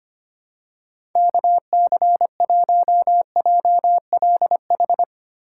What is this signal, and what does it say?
Morse code, heard as a keyed tone, about 25 wpm, 700 Hz. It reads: KC1JLH
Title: USA Call Signs - 6 Characters - 3x Character Spacing - 25wpm